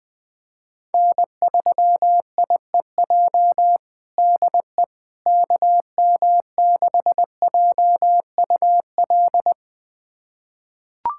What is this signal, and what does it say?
Morse code, heard as a keyed tone, about 20 wpm, 700 Hz. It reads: N3IEJ DE KM6JUL
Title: USA Callsign DE Callsign 20wpm